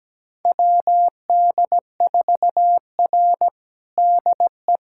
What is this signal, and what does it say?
Morse code, heard as a keyed tone, about 17 wpm, 700 Hz. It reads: WD4R DE
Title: USA Callsign DE Callsign - 4 Characters - 17wpm